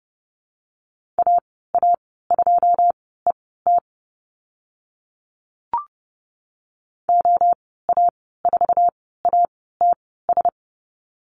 Morse code, heard as a keyed tone, about 30 wpm, spaced as for 17 wpm, 700 Hz. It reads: AA2ET OA4ATS
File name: course / 30:10wpm / 088 Lesson 15g - Number 5 - Call Signs 30wpm